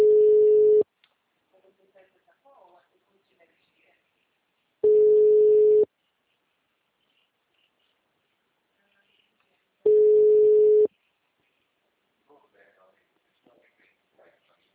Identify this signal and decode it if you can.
Ringback tone (Europe, 425 Hz), with speech.
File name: out- -2011- - - .71482